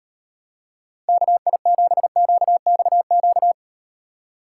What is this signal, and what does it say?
Morse code, heard as a keyed tone, about 38 wpm, 700 Hz. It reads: KI7QXQ